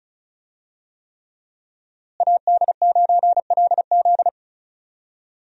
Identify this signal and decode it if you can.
Morse code, heard as a keyed tone, about 36 wpm, 700 Hz. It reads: AD9LZ
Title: USA Call Signs - 5 Characters - Repeated 3x - 35wpm